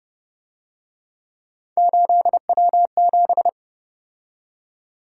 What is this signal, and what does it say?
Morse code, heard as a keyed tone, about 30 wpm, 700 Hz. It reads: 8W7